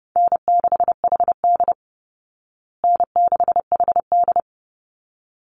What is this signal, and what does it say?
Morse code, heard as a keyed tone, about 30 wpm, 700 Hz. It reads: N6HD N6HD
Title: Call Signs Repeated 3x - 30wpm